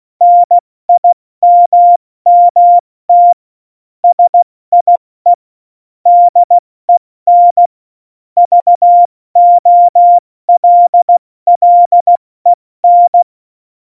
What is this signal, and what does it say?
Morse code, heard as a keyed tone, about 16 wpm, 700 Hz. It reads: NIMMT SIE DEN VOLLEN
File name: Morsekurs / Witze / Witz 17.Dezember 2019, 70 F80